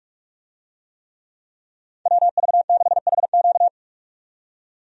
Morse code, heard as a keyed tone, about 46 wpm, 700 Hz. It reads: WV6HQ
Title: USA Call Signs - 5 Characters - 45wpm